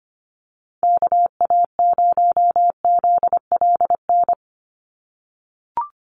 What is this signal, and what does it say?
Morse code, heard as a keyed tone, about 25 wpm, 700 Hz. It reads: KA0ZLN
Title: USA Call Signs 25wpm